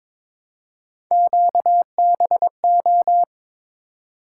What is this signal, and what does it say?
Morse code, heard as a keyed tone, about 22 wpm, 700 Hz. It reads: QBO
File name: practice / 22wpm / Three Letter-Number Combinations - 22wpm